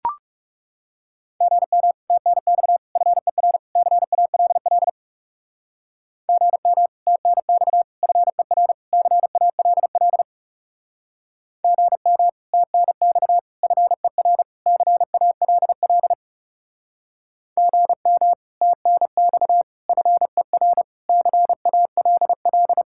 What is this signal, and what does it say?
Morse code, heard as a keyed tone, about 40 wpm, 700 Hz. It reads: GM TNX FER CALL GM TNX FER CALL GM TNX FER CALL GM TNX FER CALL